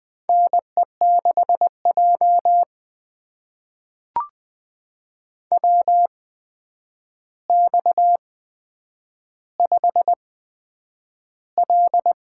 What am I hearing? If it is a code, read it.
Morse code, heard as a keyed tone, about 20 wpm, spaced as for 9 wpm, 700 Hz. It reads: NE6J WX5L
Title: USA Call Signs - 4 Characters - 8x Character Spacing - 20wpm